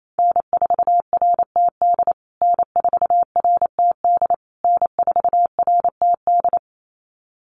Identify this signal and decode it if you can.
Morse code, heard as a keyed tone, about 28 wpm, 700 Hz. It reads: N4RTD N4RTD N4RTD
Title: USA Call Signs Repeated 3x 28wpm